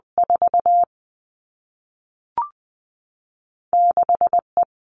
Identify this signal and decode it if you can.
Morse code, heard as a keyed tone, about 20 wpm, 700 Hz. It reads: 4 6 E
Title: All Cut Numbers - 2-Digit 20wpm